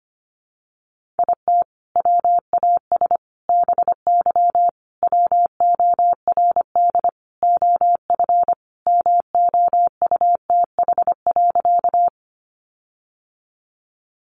Morse code, heard as a keyed tone, about 25 wpm, 700 Hz. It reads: IT WAS BY WORD OF MOUTH.
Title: Sentences from Top 1500 Words 25wpm